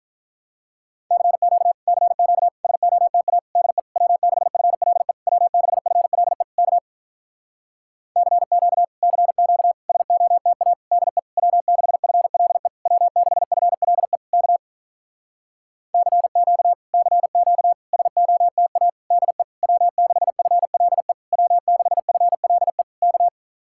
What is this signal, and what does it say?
Morse code, heard as a keyed tone, about 46 wpm, 700 Hz. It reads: CQ CQ SOTA DE W6FLE W6FLE K CQ CQ SOTA DE W6FLE W6FLE K CQ CQ SOTA DE W6FLE W6FLE K